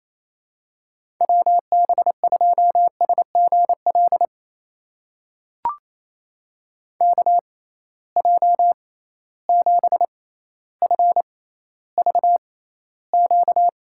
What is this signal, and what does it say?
Morse code, heard as a keyed tone, about 28 wpm, spaced as for 14 wpm, 700 Hz. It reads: WB2SGL KJ7FVQ
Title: USA Call Signs - 6 Characters - 6x Character Spacing - 28wpm